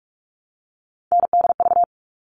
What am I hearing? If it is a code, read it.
Morse code, heard as a keyed tone, about 44 wpm, 700 Hz. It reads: NDV
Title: All Cut Numbers - 3-Digit 45wpm